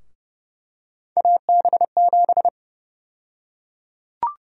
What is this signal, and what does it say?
Morse code, heard as a keyed tone, about 30 wpm, 700 Hz. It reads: AB7
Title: USA Call Sign Prefixes - 30wpm